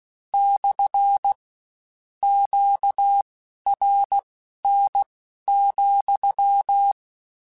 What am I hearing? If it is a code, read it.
Morse code, heard as a keyed tone, about 16 wpm, spaced as for 11 wpm, 800 Hz. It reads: / QRN,